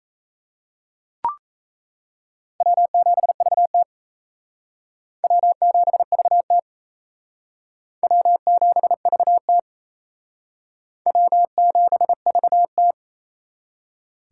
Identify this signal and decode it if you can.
Morse code, heard as a keyed tone, about 35 wpm, 700 Hz. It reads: W7VT W7VT W7VT W7VT
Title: Speed Racing - USA 1x2 Call Signs 28wpm